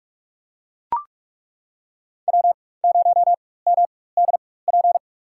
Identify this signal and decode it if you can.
Morse code, heard as a keyed tone, about 41 wpm, spaced as for 20 wpm, 700 Hz. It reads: W0MDP
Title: USA Call Signs - 5 Characters - 4x Character Spacing - 45wpm